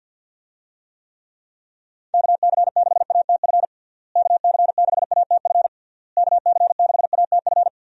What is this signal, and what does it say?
Morse code, heard as a keyed tone, about 51 wpm, 700 Hz. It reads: KC6ATF KC6ATF KC6ATF
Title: USA Call Signs - 6 Characters - Repeated 3x - 50wpm